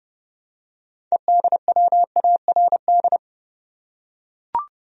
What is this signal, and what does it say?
Morse code, heard as a keyed tone, about 31 wpm, 700 Hz. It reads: EDWARD
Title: USA Common Names - 30wpm